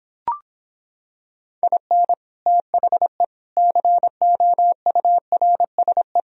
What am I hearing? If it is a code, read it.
Morse code, heard as a keyed tone, about 26 wpm, 700 Hz. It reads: IN THE COURSE